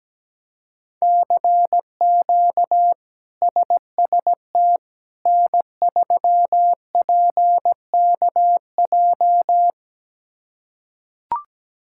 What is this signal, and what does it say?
Morse code, heard as a keyed tone, about 17 wpm, 700 Hz. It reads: CQ SST N3PKJ
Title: K1USN SST QSO Practice - 17wpm